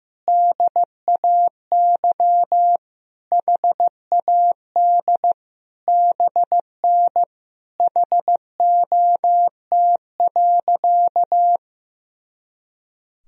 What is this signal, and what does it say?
Morse code, heard as a keyed tone, about 15 wpm, 700 Hz. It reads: DAY HAD BN HOT.
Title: Sentences from Top 500 Words and Abbreviations 15wpm